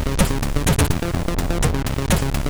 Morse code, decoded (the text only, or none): none